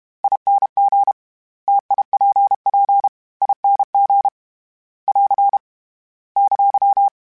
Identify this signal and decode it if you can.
Morse code, heard as a keyed tone, about 32 wpm, 800 Hz. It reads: ING TIPP ING <AR> !
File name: Variable Speed / Variable Speed-mp3s / T AOEI's sp @30